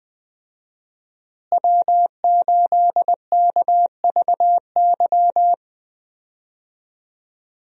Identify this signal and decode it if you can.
Morse code, heard as a keyed tone, about 21 wpm, 700 Hz. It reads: W8KVY
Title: USA Call Signs - 5 Characters - Repeated 2x - 20wpm